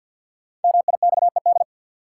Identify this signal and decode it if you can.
Morse code, heard as a keyed tone, about 51 wpm, 700 Hz. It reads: MIXED
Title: Sentences from Top 1500 Words - New Word Drill Alt - 50wpm